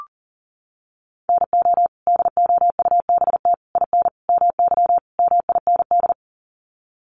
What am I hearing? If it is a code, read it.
Morse code, heard as a keyed tone, about 40 wpm, 700 Hz. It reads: NO DOUBT IN MY MIND
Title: Sets of 5 Words Encore2 40wpm